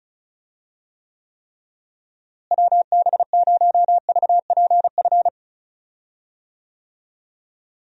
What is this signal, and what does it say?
Morse code, heard as a keyed tone, about 36 wpm, 700 Hz. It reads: WB0VPF
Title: USA Call Signs - 6 Characters - Repeated 3x - 35wpm